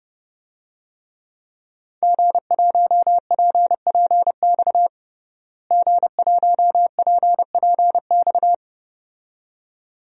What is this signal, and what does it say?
Morse code, heard as a keyed tone, about 30 wpm, 700 Hz. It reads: G1PPX G1PPX
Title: DXCC-100 Call Signs - 5 Characters - Repeated 2x - 30wpm